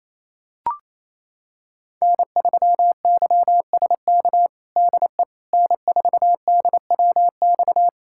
Morse code, heard as a keyed tone, about 28 wpm, 700 Hz. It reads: N3YSK DE N4DWX
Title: USA Callsign DE Callsign - 5 Characters - 28wpm